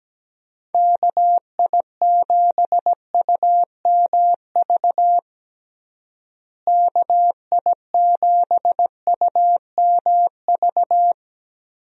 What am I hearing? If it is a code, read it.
Morse code, heard as a keyed tone, about 17 wpm, 700 Hz. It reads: KI7UMV KI7UMV
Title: USA Call Signs - 6 Characters - Repeated 3x - 17wpm